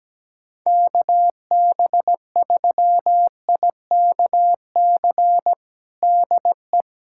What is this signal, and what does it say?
Morse code, heard as a keyed tone, about 17 wpm, 700 Hz. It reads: KB3IKC DE
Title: USA Callsign DE Callsign - 6 Characters - 17wpm